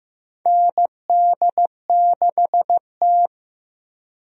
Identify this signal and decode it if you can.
Morse code, heard as a keyed tone, about 15 wpm, 700 Hz. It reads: ND6T
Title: USA Call Signs - 4 Characters - Repeated 2x - 15wpm